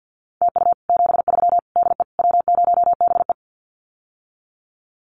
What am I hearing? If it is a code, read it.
Morse code, heard as a keyed tone, about 50 wpm, 700 Hz. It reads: TU 73 DE W0BE
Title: Common QSO Phrases Encore 50wpm